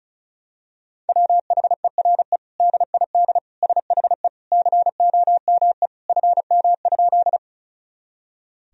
Morse code, heard as a keyed tone, about 36 wpm, 700 Hz. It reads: WHERE DID SHE COME FM?
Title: Sentences from Top 200 Words and Abbreviations 35wpm